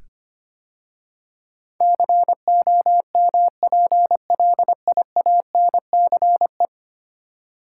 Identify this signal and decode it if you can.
Morse code, heard as a keyed tone, about 25 wpm, 700 Hz. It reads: COMPLIANCE
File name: 10 Letter Words - First 3 Letters Given - 25wpm